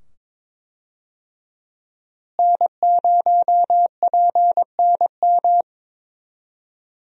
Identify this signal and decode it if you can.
Morse code, heard as a keyed tone, about 22 wpm, 700 Hz. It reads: N0PNM
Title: USA Call Signs - 5 Characters - Repeated 2x - 22wpm